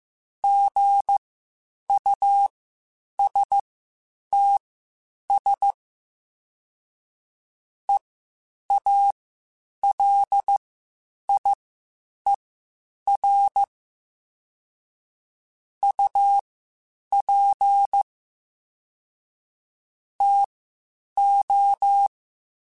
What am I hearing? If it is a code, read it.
Morse code, heard as a keyed tone, about 15 wpm, spaced as for 8 wpm, 800 Hz. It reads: GUSTS EALIER UP TO